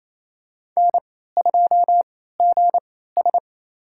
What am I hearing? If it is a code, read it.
Morse code, heard as a keyed tone, about 28 wpm, spaced as for 16 wpm, 700 Hz. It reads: N2GS
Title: USA Call Signs - 4 Characters - 3x Character Spacing - 28wpm